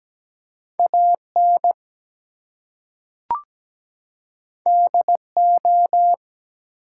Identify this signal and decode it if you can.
Morse code, heard as a keyed tone, about 17 wpm, 700 Hz. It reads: AN DO
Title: Two Letter Words - 17wpm